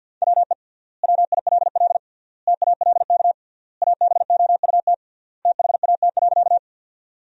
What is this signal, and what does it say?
Morse code, heard as a keyed tone, about 52 wpm, 700 Hz. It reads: WE WILL TALK ABOUT THAT.
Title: 3x Word-Spacing - Sentences from Top 400 Words - 50wpm